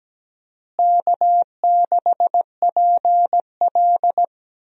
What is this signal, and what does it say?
Morse code, heard as a keyed tone, about 17 wpm, 700 Hz. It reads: K6PL
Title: USA Call Signs - 4 Characters - Repeated 2x - 17wpm